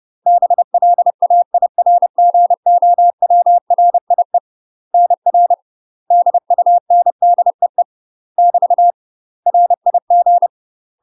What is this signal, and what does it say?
Morse code, heard as a keyed tone, about 30 wpm, 700 Hz. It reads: BLAIRGOWRIE NR DUNDEE <BT> RIG
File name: UK MASTER QSOs TRAINING / 30WPM / UK MASTER QSOs 0038